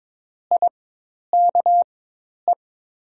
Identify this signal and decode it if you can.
Morse code, heard as a keyed tone, about 22 wpm, spaced as for 10 wpm, 690 Hz. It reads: IKE
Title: 4x Character Spacing - Top 100 Words - 22wpm